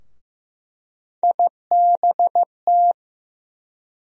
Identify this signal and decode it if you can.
Morse code, heard as a keyed tone, about 15 wpm, 700 Hz. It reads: IBT
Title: USA Call Sign Postfixes - 3 Letters 15wpm